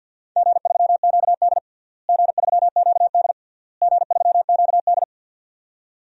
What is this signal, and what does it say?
Morse code, heard as a keyed tone, about 46 wpm, 700 Hz. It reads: G3QD G3QD G3QD
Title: DXCC-100 Call Signs - 4 Characters - Repeated 3x - 50wpm